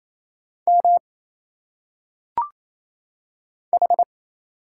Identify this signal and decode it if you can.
Morse code, heard as a keyed tone, about 27 wpm, 700 Hz. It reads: M H
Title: Single Letters - Phonetic Pronunciation 28wpm